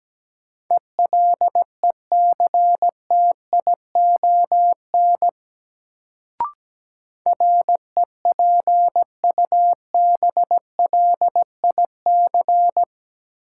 Morse code, heard as a keyed tone, about 17 wpm, 700 Hz. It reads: ELECTION REPUBLIC
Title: Sentences from Top 1500 Words - New Word Drill - 17wpm